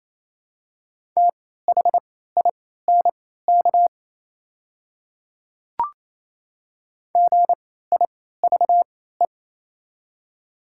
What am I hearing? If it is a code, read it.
Morse code, heard as a keyed tone, about 28 wpm, spaced as for 16 wpm, 700 Hz. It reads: THINK GIVE
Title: 3x Character Spacing - Top 500 Words - 28wpm